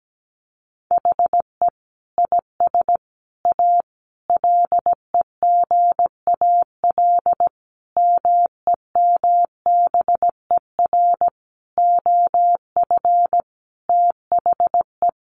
Morse code, heard as a keyed tone, about 17 wpm, 700 Hz. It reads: HE IS A LEGAL MEMBER OF THE
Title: Sentences from Top 1200 Words 17wpm